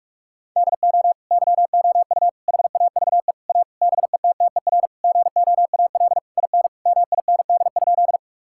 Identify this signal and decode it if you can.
Morse code, heard as a keyed tone, about 41 wpm, 700 Hz. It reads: DO YOU HAVE A BETTER GOAL IN MIND?
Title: Sentences from Top 1000 Words - Review Part 5 - 45wpm